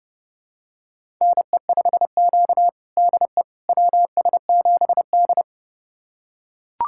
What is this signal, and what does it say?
Morse code, heard as a keyed tone, about 30 wpm, 700 Hz. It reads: NE5Q DE WS7D
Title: USA Callsign DE Callsign - 4 Characters - 30wpm